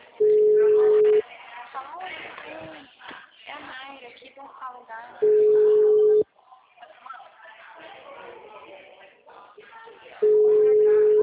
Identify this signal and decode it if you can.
Ringback tone (Europe, 425 Hz), with speech.